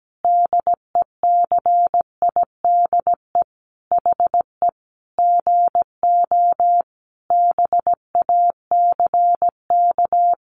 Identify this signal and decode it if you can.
Morse code, heard as a keyed tone, about 17 wpm, 700 Hz. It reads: DECIDE HE GO BACK
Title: Sentences from Top 500 Words 17wpm